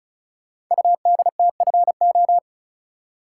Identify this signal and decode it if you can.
Morse code, heard as a keyed tone, about 34 wpm, 700 Hz. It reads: UDTFO
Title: Code Groups - 5 Letters - 35wpm